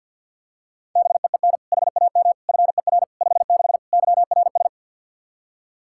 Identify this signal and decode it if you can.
Morse code, heard as a keyed tone, about 49 wpm, 700 Hz. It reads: BEEN HAM FER 56 YRS
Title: Practice QSO - Exchange 3 - More Info - 50wpm